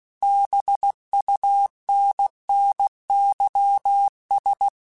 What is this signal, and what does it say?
Morse code, heard as a keyed tone, about 16 wpm, 800 Hz. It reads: BUNNYS
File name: Misc / Misc-mp3s / BIG @15